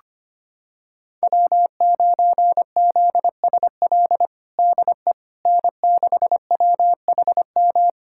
Morse code, heard as a keyed tone, about 25 wpm, 700 Hz. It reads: W9ZSL DE N6WHM
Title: USA Callsign DE Callsign - 5 Characters - 25wpm